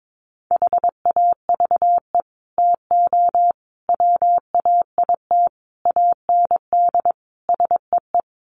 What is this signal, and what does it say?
Morse code, heard as a keyed tone, about 22 wpm, 700 Hz. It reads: HAVE TO WAIT AND SEE
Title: Sets of 5 Words Encore2 22wpm